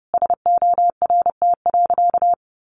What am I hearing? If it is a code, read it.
Morse code, heard as a keyed tone, about 30 wpm, 700 Hz. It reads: SORT.
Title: Sentences from Top 1300 Words 30wpm